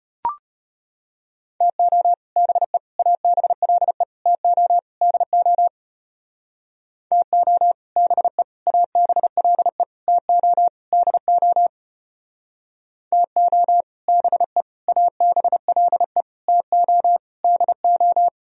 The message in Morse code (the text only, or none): TO BE ABLE TO DO TO BE ABLE TO DO TO BE ABLE TO DO